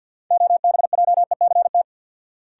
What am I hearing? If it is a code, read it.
Morse code, heard as a keyed tone, about 50 wpm, 700 Hz. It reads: OBJECT
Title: Sentences from Top 500 Words - New Word Drill - 50wpm